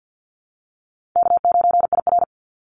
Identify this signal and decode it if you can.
Morse code, heard as a keyed tone, about 50 wpm, 700 Hz. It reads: K9IR
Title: NA SOTA Activator Callsigns 50wpm